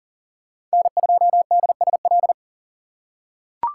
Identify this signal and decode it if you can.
Morse code, heard as a keyed tone, about 40 wpm, 690 Hz. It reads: N2DSL
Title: USA Call Signs - 5 Characters - 3x Character Spacing - 40wpm